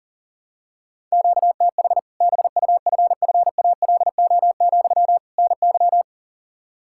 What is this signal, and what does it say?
Morse code, heard as a keyed tone, about 40 wpm, 700 Hz. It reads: QTH BUFFALO, NY
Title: Practice QSO - Exchange 1 - Beginning - 40wpm